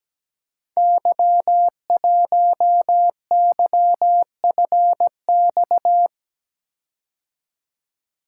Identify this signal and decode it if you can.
Morse code, heard as a keyed tone, about 17 wpm, 700 Hz. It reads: Y1YFX
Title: Code Groups - 5 Letter:Numbers - 17wpm